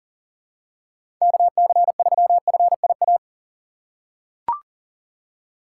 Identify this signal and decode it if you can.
Morse code, heard as a keyed tone, about 37 wpm, 700 Hz. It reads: KC3FIA
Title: USA Call Signs - 6 Characters - 4x Character Spacing - 40wpm